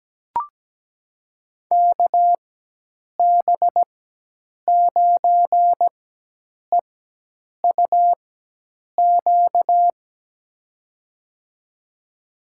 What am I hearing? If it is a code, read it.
Morse code, heard as a keyed tone, about 17 wpm, spaced as for 8 wpm, 700 Hz. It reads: KB9EUQ
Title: USA Call Signs - 4x Character Spacing - 17wpm